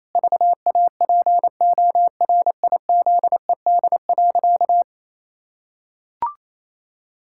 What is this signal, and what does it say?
Morse code, heard as a keyed tone, about 28 wpm, 700 Hz. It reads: VAPORIZED.
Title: Pangram Challenge v2 - 28wpm